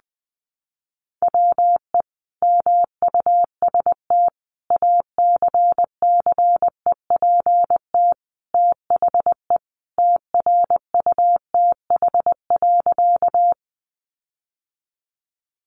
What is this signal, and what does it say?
Morse code, heard as a keyed tone, about 20 wpm, 700 Hz. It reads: WE MUST ACCEPT THE TRUTH.
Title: Sentences from Top 1300 Words 20wpm